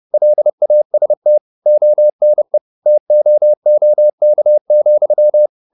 Morse code, heard as a keyed tone, about 30 wpm, 600 Hz. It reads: LAST ONE TOOK,